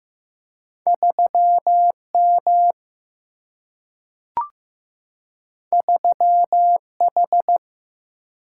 Morse code, heard as a keyed tone, about 15 wpm, 700 Hz. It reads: 3M 3H
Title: Single-Letter Single-Number 15wpm